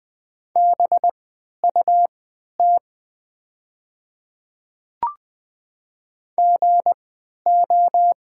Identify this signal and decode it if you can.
Morse code, heard as a keyed tone, about 20 wpm, spaced as for 11 wpm, 700 Hz. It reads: BUT GO
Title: 3x Character Spacing - Top 100 Words - 20wpm